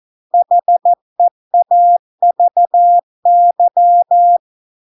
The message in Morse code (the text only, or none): HEAVY